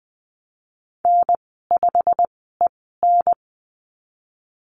Morse code, heard as a keyed tone, about 19 wpm, spaced as for 14 wpm, 700 Hz. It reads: N5EN
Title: 122 Lesson 20g - Number 3 - Call Signs 20wpm